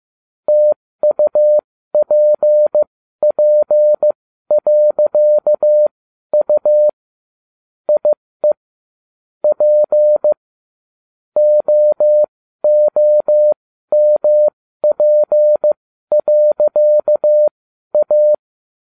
Morse code, heard as a keyed tone, about 15 wpm, spaced as for 12 wpm, 600 Hz. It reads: TUPP.U IE P OOMP.A